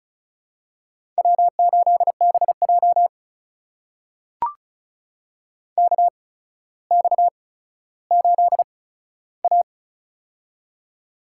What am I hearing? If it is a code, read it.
Morse code, heard as a keyed tone, about 36 wpm, spaced as for 15 wpm, 700 Hz. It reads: W8BJ KX8A
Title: USA Call Signs - 4 Characters - 8x Character Spacing - 35wpm